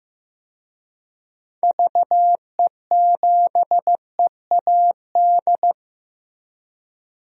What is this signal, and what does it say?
Morse code, heard as a keyed tone, about 15 wpm, 700 Hz. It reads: VE7EAD
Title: DXCC-100 Call Signs - 6 Characters - Repeated 3x - 15wpm